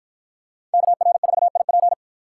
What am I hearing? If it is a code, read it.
Morse code, heard as a keyed tone, about 51 wpm, 700 Hz. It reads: KR4IP